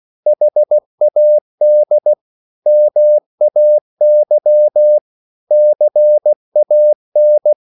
Morse code, heard as a keyed tone, about 16 wpm, 600 Hz. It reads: HAD MAY CAN
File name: CW / Practice / 500MostCommonEnglishWords16